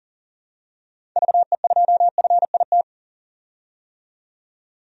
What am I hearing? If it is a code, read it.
Morse code, heard as a keyed tone, about 40 wpm, 700 Hz. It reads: VE2FIT
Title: DXCC-100 Call Signs - 6 Characters - 40wpm